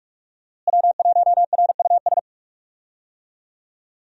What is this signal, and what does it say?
Morse code, heard as a keyed tone, about 45 wpm, 700 Hz. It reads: W1RUS
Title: USA Call Signs - 5 Characters - Repeated 2x - 45wpm